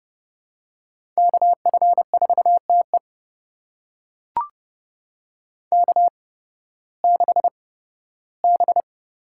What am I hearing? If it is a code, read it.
Morse code, heard as a keyed tone, about 29 wpm, spaced as for 16 wpm, 700 Hz. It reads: KF4TE K6B
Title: USA Call Signs - 5 Characters - 8x Character Spacing - 30wpm